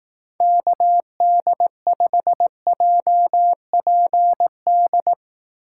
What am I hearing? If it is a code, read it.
Morse code, heard as a keyed tone, about 18 wpm, 700 Hz. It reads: KD5JPD
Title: Speed Racing - USA 2x3 Call Signs 15wpm